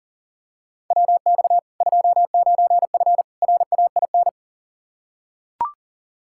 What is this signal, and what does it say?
Morse code, heard as a keyed tone, about 40 wpm, 700 Hz. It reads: WX 29F RAIN
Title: K1USN SST QSO Practice 40wpm